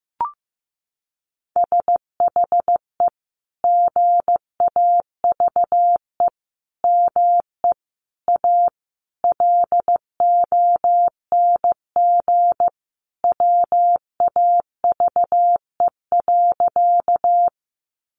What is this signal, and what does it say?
Morse code, heard as a keyed tone, about 15 wpm, 700 Hz. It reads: SHE GAVE ME A LONG WAVE.
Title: Sentences from Top 1400 Words 15wpm